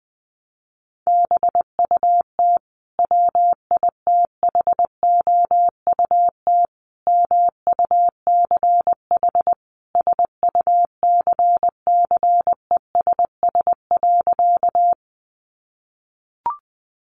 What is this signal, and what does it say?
Morse code, heard as a keyed tone, about 19 wpm, 700 Hz. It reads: BUT WITHOUT MUCH SUCCESS.
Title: Sentences from Top 900 Words 20wpm